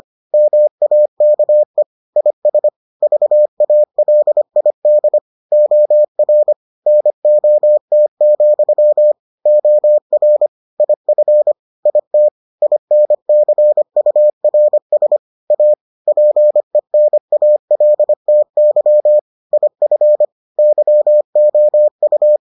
Morse code, heard as a keyed tone, about 25 wpm, 600 Hz. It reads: MAKE IS VALID OR NOT, OR IF IT INCURS A PENALTY IF YOU